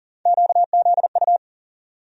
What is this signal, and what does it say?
Morse code, heard as a keyed tone, about 40 wpm, 700 Hz. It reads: QZU
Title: USA Call Sign Postfixes - 3 Letters 40wpm